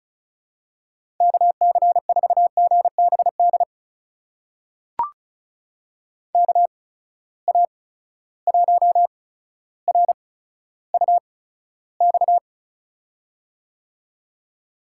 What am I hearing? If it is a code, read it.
Morse code, heard as a keyed tone, about 36 wpm, spaced as for 15 wpm, 700 Hz. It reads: KC4GBD KA1RUX
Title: USA Call Signs - 6 Characters - 8x Character Spacing - 35wpm